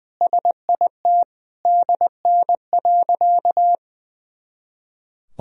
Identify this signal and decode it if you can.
Morse code, heard as a keyed tone, about 21 wpm, 700 Hz. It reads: SIT DN.
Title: Sentences from Top 500 Words and Abbreviations 20wpm